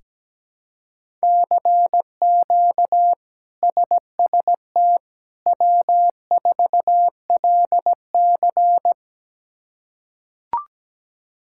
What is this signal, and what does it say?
Morse code, heard as a keyed tone, about 17 wpm, 700 Hz. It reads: CQ SST W4LC
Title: K1USN SST QSO Practice - 17wpm